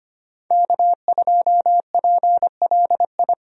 Morse code, heard as a keyed tone, about 25 wpm, 700 Hz. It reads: K2PLI
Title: USA Call Signs - 5 Characters - 25wpm